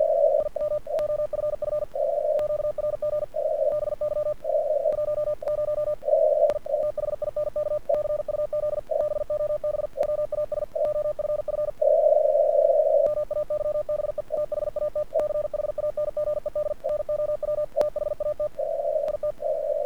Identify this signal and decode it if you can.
Morse code, heard as a keyed tone, about 48 wpm, 1200 Hz. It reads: IM OFF QRG BY 10 I THINK CUZ BOB WAS OFF MAYBE THAT CHANGED NOW THAT IT
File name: archived / CRSnet